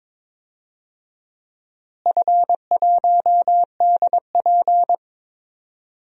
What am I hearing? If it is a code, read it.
Morse code, heard as a keyed tone, about 22 wpm, 700 Hz. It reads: F1DP
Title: DXCC-100 Call Signs - 4 Characters - Repeated 2x - 22wpm